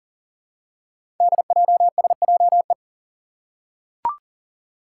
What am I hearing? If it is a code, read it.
Morse code, heard as a keyed tone, about 42 wpm, 700 Hz. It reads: DJSJE
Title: Five Challenging Letter-Number Combinations 40wpm